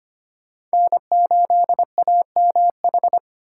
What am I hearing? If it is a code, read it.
Morse code, heard as a keyed tone, about 25 wpm, 700 Hz. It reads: N8AMH
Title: USA Call Signs - 5 Characters - 4x Character Spacing - 25wpm